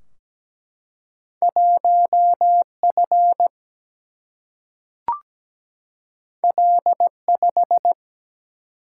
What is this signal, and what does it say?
Morse code, heard as a keyed tone, about 17 wpm, 700 Hz. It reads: 1 F L 5
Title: Single-Letter Single-Number 17wpm